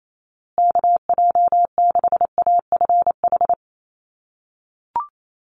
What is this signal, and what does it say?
Morse code, heard as a keyed tone, about 28 wpm, 700 Hz. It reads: KJ6AFH
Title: USA Call Signs Repeated 3x 28wpm